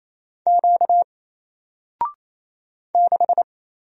Morse code, heard as a keyed tone, about 29 wpm, 700 Hz. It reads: Q 6
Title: Single Letter-Number 28wpm